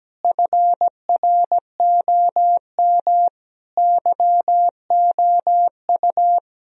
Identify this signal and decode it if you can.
Morse code, heard as a keyed tone, about 17 wpm, 700 Hz. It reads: FROM YOU
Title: Sentences from Top 1000 Words - Review Part 4 - 17wpm